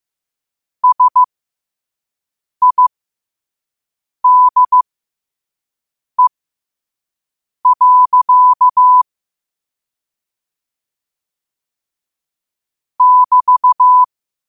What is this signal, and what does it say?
Morse code, heard as a keyed tone, about 15 wpm, spaced as for 5 wpm, 1000 Hz. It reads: SIDE. <BT>
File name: qotdfiles / 5wpm